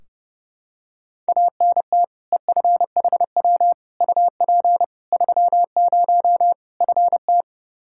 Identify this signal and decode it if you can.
Morse code, heard as a keyed tone, about 30 wpm, 700 Hz. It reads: ANT EFHW UP 30 FT
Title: Practice QSO - Exchange 2 - Info - 30wpm